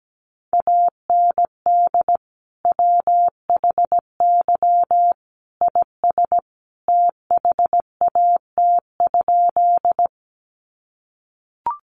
Morse code, heard as a keyed tone, about 17 wpm, 700 Hz. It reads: AND WHY IS THAT?
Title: Sentences from Top 200 Words 17wpm